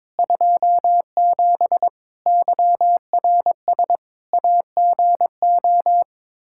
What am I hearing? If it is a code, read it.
Morse code, heard as a keyed tone, about 22 wpm, 700 Hz. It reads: 27 YRS AGO
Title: K1USN SST QSO Practice 22wpm